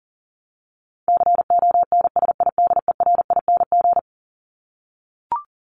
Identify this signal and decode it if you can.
Morse code, heard as a keyed tone, about 40 wpm, 700 Hz. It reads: CONSIDERING
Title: ING Suffix - Top 100 Words with Base Words 40wpm